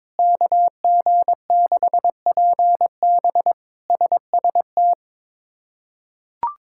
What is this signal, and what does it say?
Morse code, heard as a keyed tone, about 22 wpm, 700 Hz. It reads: KG6PB SST
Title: K1USN SST QSO Practice - 22wpm